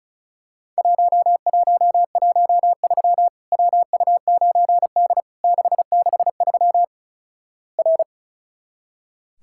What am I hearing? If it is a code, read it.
Morse code, heard as a keyed tone, about 36 wpm, 700 Hz. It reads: 1113 WU9D 663 R
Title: Worked All Europe DX Contest - QTC Exchange - 35wpm